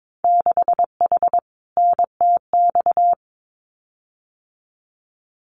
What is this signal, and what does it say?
Morse code, heard as a keyed tone, about 22 wpm, 700 Hz. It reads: 6H NTX
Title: Winter Field Day Exchanges Encore 22wpm